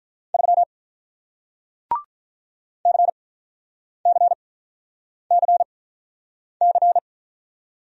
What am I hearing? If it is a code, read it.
Morse code, heard as a keyed tone, about 48 wpm, spaced as for 9 wpm, 700 Hz. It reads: 3 CCCC